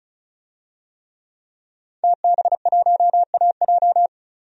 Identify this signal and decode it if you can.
Morse code, heard as a keyed tone, about 36 wpm, 700 Hz. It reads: TB1AJ
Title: DXCC-100 Call Signs - 5 Characters - 35wpm